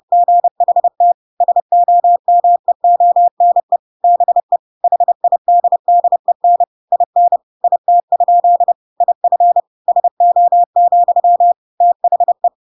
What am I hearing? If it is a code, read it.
Morse code, heard as a keyed tone, about 30 wpm, 700 Hz. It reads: GHT SOMEONE BE HIDDEN IN IT? IF SO, THE
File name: Pinocchio / Pinocchio ch1 30 30 ik1hge